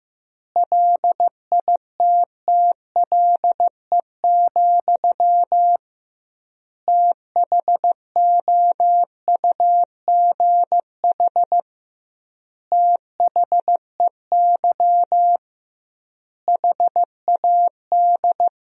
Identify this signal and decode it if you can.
Morse code, heard as a keyed tone, about 15 wpm, 700 Hz. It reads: LITTLE, THOUGH THEY HAD
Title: 2x Word-Spacing - Sentences from Top 400 Words - 15wpm